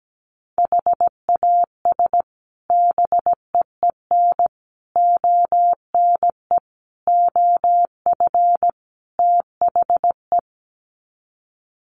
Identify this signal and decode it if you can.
Morse code, heard as a keyed tone, about 17 wpm, 700 Hz. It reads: HAS BEEN ONE OF THE
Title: Sets of 5 Words Encore 17wpm